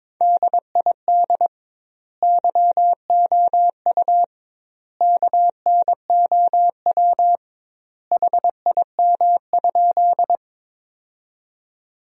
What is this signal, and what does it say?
Morse code, heard as a keyed tone, about 22 wpm, 700 Hz. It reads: DID YOU KNOW HIM?
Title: 2x Word-Spacing - Sentences from Top 100 Words - 22wpm